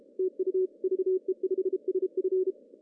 Morse code, heard as a keyed tone, about 34 wpm, 370 Hz. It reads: TU VE5SF
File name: CW / cqww-cw-k5tr